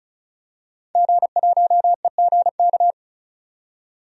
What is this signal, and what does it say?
Morse code, heard as a keyed tone, about 34 wpm, 700 Hz. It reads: G1EGK
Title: DXCC-100 Call Signs - 5 Characters - 35wpm